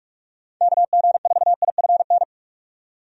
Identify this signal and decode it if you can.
Morse code, heard as a keyed tone, about 45 wpm, 700 Hz. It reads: KG4IFN
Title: USA Call Signs - 6 Characters - 45wpm